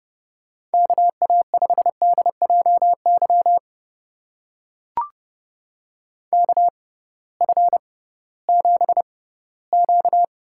Morse code, heard as a keyed tone, about 30 wpm, spaced as for 18 wpm, 700 Hz. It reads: KA5DJY KF7Q
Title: USA Call Signs - 6x Character Spacing - 30wpm